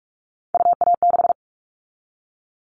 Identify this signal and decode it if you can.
Morse code, heard as a keyed tone, about 46 wpm, 700 Hz. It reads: UA6
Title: Callsign Prefixes for Top 25 DXCC - 45wpm